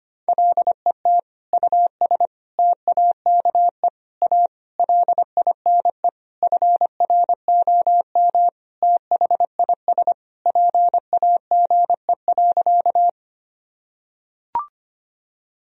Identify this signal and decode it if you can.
Morse code, heard as a keyed tone, about 25 wpm, 700 Hz. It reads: LET US TAKE A LINE FROM THIS PAGE.
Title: Sentences from Top 1000 Words - Review Part 1 - 25wpm